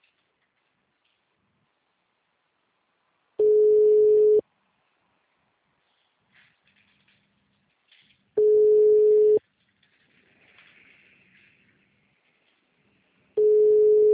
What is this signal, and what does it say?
Ringback tone (Europe, 425 Hz).
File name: out- -2028- - - .6696